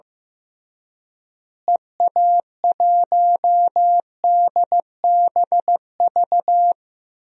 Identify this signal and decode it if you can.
Morse code, heard as a keyed tone, about 15 wpm, 700 Hz. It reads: EA1DBV
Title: DXCC-100 Call Signs - 6 Characters - Repeated 3x - 15wpm